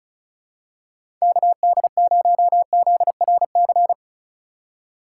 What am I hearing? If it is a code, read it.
Morse code, heard as a keyed tone, about 35 wpm, 690 Hz. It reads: KD0ZRC
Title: USA Call Signs - 6 Characters - 6x Character Spacing - 35wpm